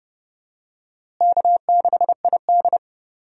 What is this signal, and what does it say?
Morse code, heard as a keyed tone, about 30 wpm, 700 Hz. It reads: K6ID
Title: USA Call Signs - 4 Characters - Repeated 2x - 30wpm